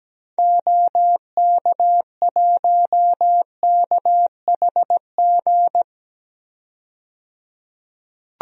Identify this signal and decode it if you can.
Morse code, heard as a keyed tone, about 17 wpm, 700 Hz. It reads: OK1KHG
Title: DXCC-100 Call Signs - 6 Characters - 17wpm